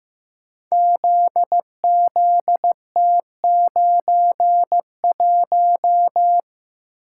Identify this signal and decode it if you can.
Morse code, heard as a keyed tone, about 15 wpm, 700 Hz. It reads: ZZT91
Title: Code Groups - 5 Letter:Numbers - 15wpm